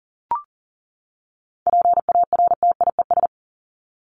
Morse code, heard as a keyed tone, about 40 wpm, 700 Hz. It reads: PARTIES
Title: ES:S Suffix - Top 100 Words with Base Words 40wpm